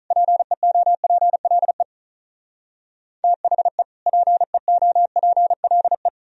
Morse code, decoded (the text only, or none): PEOPLE THE PEOPLE